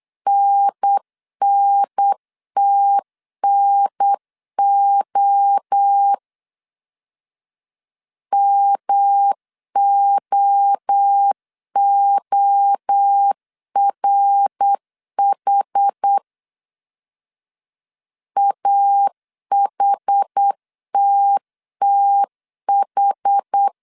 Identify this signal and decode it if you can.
Morse code, heard as a keyed tone, about 8 wpm, 800 Hz. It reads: NNTNO MOORH AHTTH